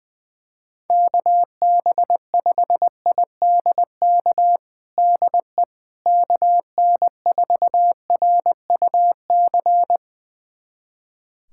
Morse code, heard as a keyed tone, about 20 wpm, 700 Hz. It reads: KB5IDK DE KN4RUC
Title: USA Callsign DE Callsign - 6 Characters - 20wpm